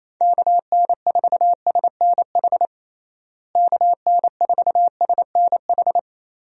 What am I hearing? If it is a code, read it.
Morse code, heard as a keyed tone, about 28 wpm, 700 Hz. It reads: KN4SNH KN4SNH
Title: USA Call Signs - 6 Characters - Repeated 2x - 28wpm